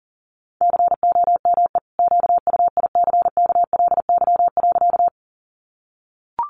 Morse code, heard as a keyed tone, about 40 wpm, 700 Hz. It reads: COME QUICKLY.
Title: Sentences from Top 700 Words 40wpm